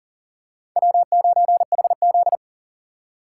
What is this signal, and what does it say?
Morse code, heard as a keyed tone, about 40 wpm, 700 Hz. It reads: W9HZ
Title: USA Call Signs - 4 Characters - 3x Character Spacing - 40wpm